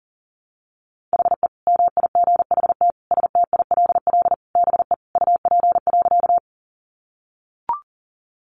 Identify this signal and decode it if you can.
Morse code, heard as a keyed tone, about 40 wpm, 700 Hz. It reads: HE MIGHT STILL BE UP.
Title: Sentences from Top 300 Words 40wpm